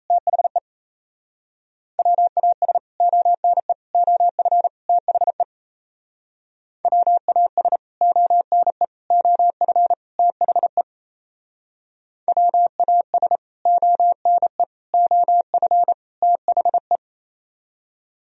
Morse code, heard as a keyed tone, about 32 wpm, 700 Hz. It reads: THE WAS ONE OF THE WAS ONE OF THE WAS ONE OF THE
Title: Speed-Racing - Sets of 4 Words 28wpm